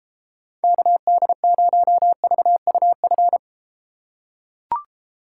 Morse code, heard as a keyed tone, about 33 wpm, 700 Hz. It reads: KD0VUF